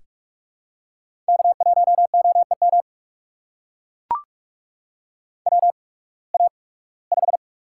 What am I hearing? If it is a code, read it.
Morse code, heard as a keyed tone, about 44 wpm, spaced as for 23 wpm, 690 Hz. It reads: K1OEM WA5
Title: USA Call Signs - 8x Character Spacing - 45wpm